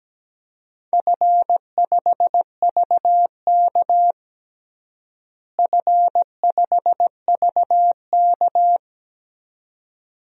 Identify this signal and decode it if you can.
Morse code, heard as a keyed tone, about 17 wpm, 700 Hz. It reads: F5VK F5VK
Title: DXCC-100 Call Signs - 4 Characters - Repeated 2x - 17wpm